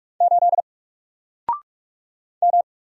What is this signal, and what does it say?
Morse code, heard as a keyed tone, about 45 wpm, 700 Hz. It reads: 8 M
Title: Single Letter-Number 45wpm